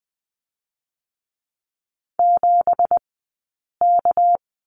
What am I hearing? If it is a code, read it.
Morse code, heard as a keyed tone, about 20 wpm, 700 Hz. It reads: 7 K
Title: 198 Lesson 32e - Letter k - Three Characters 20wpm